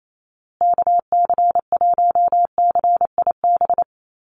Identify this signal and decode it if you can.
Morse code, heard as a keyed tone, about 28 wpm, 700 Hz. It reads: KC1CIB
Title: USA Call Signs Repeated 3x 28wpm